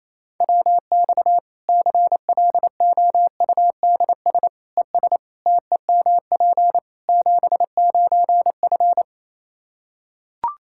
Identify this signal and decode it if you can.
Morse code, heard as a keyed tone, about 27 wpm, 700 Hz. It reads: WX CLOUDS ES TEMP 79F
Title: Practice QSO - Exchange 2 - Info - 28wpm